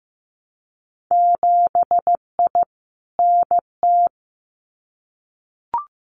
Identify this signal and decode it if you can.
Morse code, heard as a keyed tone, about 15 wpm, 700 Hz. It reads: 7I NT
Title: Winter Field Day Exchanges Encore 15wpm